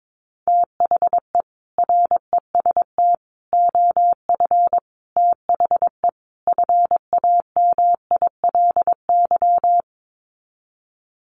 Morse code, heard as a keyed tone, about 22 wpm, 700 Hz. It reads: THE REST OF THE FAMILY
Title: Sets of 5 Words Encore2 22wpm